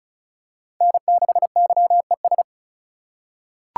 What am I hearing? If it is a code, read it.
Morse code, heard as a keyed tone, about 34 wpm, 700 Hz. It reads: N6YES
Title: USA Call Signs - 5 Characters - 3x Character Spacing - 35wpm